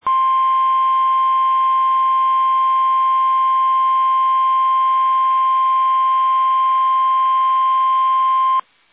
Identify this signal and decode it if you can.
Emergency Alert System (SAME) tones (the 1050 Hz attention tone). It is heard from 0.0 to 8.7 s.
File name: eas / KUSO / KUSO